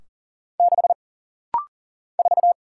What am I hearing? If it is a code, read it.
Morse code, heard as a keyed tone, about 40 wpm, 700 Hz. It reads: 6 4
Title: Single Letter-Number - Rapid-Fire 40wpm